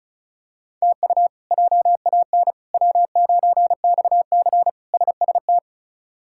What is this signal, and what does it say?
Morse code, heard as a keyed tone, about 34 wpm, 700 Hz. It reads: TU JAN W9XC SST
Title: K1USN SST QSO Practice - 35wpm